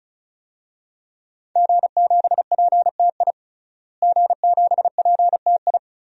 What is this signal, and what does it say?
Morse code, heard as a keyed tone, about 34 wpm, 700 Hz. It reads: G7PTI G7PTI
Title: DXCC-100 Call Signs - 5 Characters - Repeated 3x - 35wpm